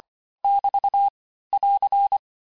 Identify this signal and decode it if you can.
Morse code, heard as a keyed tone, about 25 wpm, 800 Hz. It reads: <BT> <AR>
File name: texte-chiffre-vitesse-23mpm